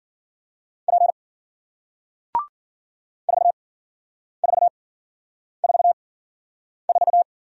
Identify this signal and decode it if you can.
Morse code, heard as a keyed tone, about 54 wpm, spaced as for 9 wpm, 700 Hz. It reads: P 4444